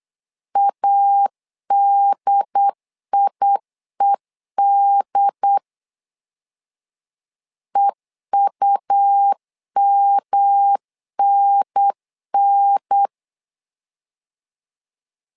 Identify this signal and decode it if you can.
Morse code, heard as a keyed tone, about 8 wpm, 800 Hz. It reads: ADIED EUMNN
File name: LEZIONI / 3 MWRO / 3 MW 8wpm 2501